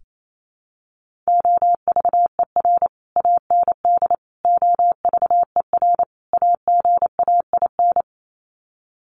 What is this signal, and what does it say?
Morse code, heard as a keyed tone, about 28 wpm, 700 Hz. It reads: OVER AND OVER AGAIN
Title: Sets of 4 Words 28wpm